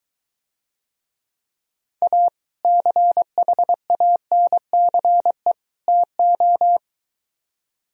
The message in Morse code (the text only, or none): A CHANCE TO